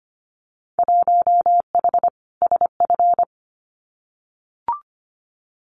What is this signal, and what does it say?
Morse code, heard as a keyed tone, about 25 wpm, 700 Hz. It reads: 1H SF
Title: Winter Field Day Exchanges 25wpm